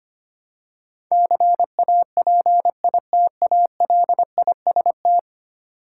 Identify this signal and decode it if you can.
Morse code, heard as a keyed tone, about 25 wpm, 700 Hz. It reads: CAPITALIST
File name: Ten Letter Words - 25wpm